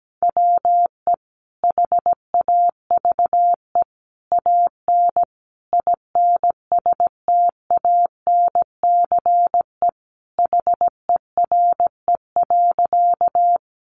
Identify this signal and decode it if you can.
Morse code, heard as a keyed tone, about 17 wpm, 700 Hz. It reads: WE HAVE AN INSTANCE HERE.
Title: Sentences from Top 800 Words 17wpm